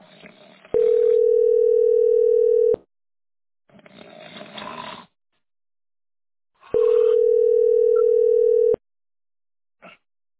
Ringback tone (North America), 440 + 480 Hz.